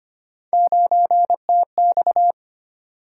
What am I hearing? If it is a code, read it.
Morse code, heard as a keyed tone, about 25 wpm, 700 Hz. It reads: 9TX
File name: Three Letter-Number Combinations - 25wpm